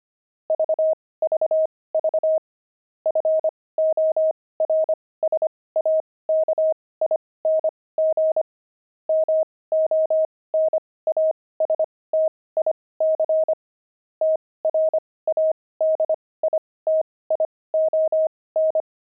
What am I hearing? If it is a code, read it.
Morse code, heard as a keyed tone, about 25 wpm, spaced as for 18 wpm, 630 Hz. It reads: VVV FORSAKING MONASTIC TRADITION
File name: pangram-fs-18-cs-25-friars